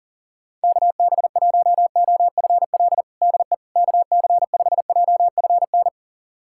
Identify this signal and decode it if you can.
Morse code, heard as a keyed tone, about 40 wpm, 700 Hz. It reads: KB1OFL DE KC5JFN
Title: USA Callsign DE Callsign - 6 Characters - 40wpm